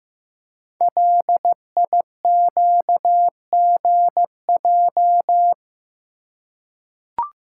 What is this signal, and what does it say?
Morse code, heard as a keyed tone, about 15 wpm, 690 Hz. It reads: LIQGJ
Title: Code Groups - 5 Letters - 15wpm